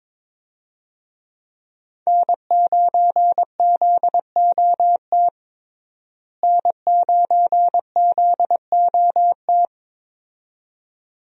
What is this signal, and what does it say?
Morse code, heard as a keyed tone, about 22 wpm, 700 Hz. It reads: N9ZOT N9ZOT
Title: USA Call Signs - 5 Characters - Repeated 2x - 22wpm